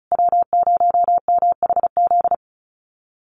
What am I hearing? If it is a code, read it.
Morse code, heard as a keyed tone, about 34 wpm, 700 Hz. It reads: W0MHZ
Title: USA Call Signs Repeated 3x 35wpm